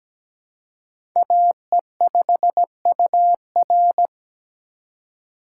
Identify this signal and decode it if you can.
Morse code, heard as a keyed tone, about 17 wpm, 700 Hz. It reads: AE5UR
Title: USA Call Signs - 5 Characters - 3x Character Spacing - 17wpm